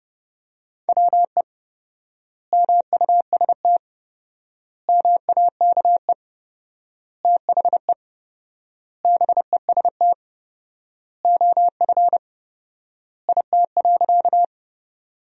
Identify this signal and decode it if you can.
Morse code, heard as a keyed tone, about 32 wpm, 700 Hz. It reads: WE MUST MAKE THE BEST OF IT.
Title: 4x Word-Spacing - Sentences from Top 400 Words - 30wpm